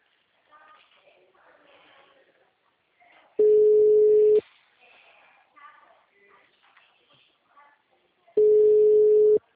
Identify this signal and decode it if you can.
Ringback tone (Europe, 425 Hz), with speech.